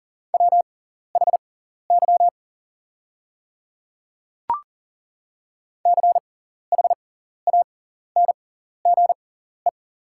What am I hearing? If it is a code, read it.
Morse code, heard as a keyed tone, about 40 wpm, spaced as for 14 wpm, 690 Hz. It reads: WHY CHANGE
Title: 6x Character Spacing - Top 200 Words - 40wpm